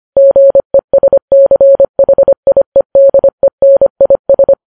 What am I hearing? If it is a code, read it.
Morse code, heard as a keyed tone, about 25 wpm, 550 Hz. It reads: GESCHIEDENIS